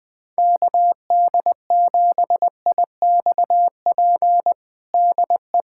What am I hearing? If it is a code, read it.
Morse code, heard as a keyed tone, about 20 wpm, 700 Hz. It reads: KD7IXP DE
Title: USA Callsign DE Callsign 20wpm